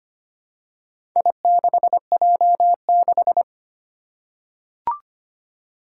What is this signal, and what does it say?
Morse code, heard as a keyed tone, about 25 wpm, 700 Hz. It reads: I6J6
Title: Four Challenging Letter-Number Combinations 25wpm